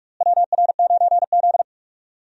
Morse code, heard as a keyed tone, about 46 wpm, 700 Hz. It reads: WR9Z
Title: USA Call Signs - 4 Characters - Repeated 2x - 45wpm